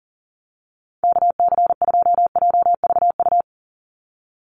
Morse code, heard as a keyed tone, about 40 wpm, 700 Hz. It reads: KC2JVU
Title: USA Call Signs 40wpm